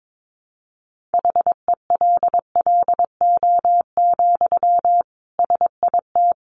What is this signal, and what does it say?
Morse code, heard as a keyed tone, about 22 wpm, 700 Hz. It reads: HELLO, SIT
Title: Sentences from Top 1100 Words 22wpm